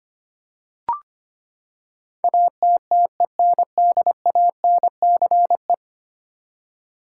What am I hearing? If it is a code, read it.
Morse code, heard as a keyed tone, about 25 wpm, 700 Hz. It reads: ATTENDANCE
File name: Ten Letter Words - 25wpm